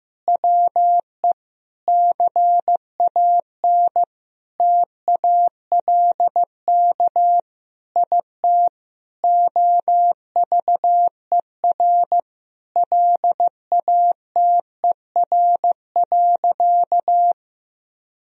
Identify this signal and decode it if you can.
Morse code, heard as a keyed tone, about 15 wpm, 700 Hz. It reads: WE CAN TALK IT OVER LATER.
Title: Sentences from Top 1000 Words - Review Part 4 - 15wpm